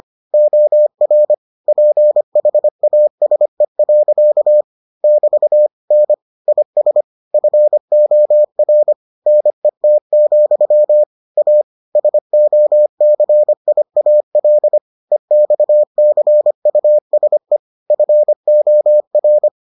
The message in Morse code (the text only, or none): OR PHASE. <BT> N IS FOR NET, A SOCIAL EXCUSE FOR